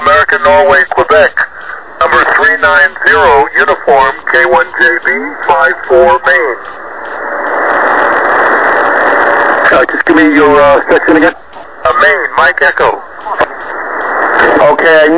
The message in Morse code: none